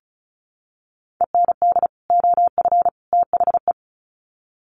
Morse code, heard as a keyed tone, about 34 wpm, 700 Hz. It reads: END OF THE
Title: Sets of 3 Words 35wpm